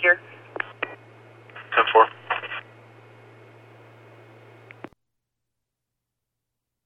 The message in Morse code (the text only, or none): none